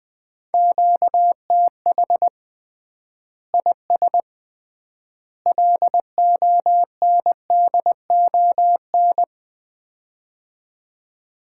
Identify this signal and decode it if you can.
Morse code, heard as a keyed tone, about 21 wpm, 700 Hz. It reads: QTH IS LONDON
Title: Common QSO Phrases - 3x Word-Spacing - 20wpm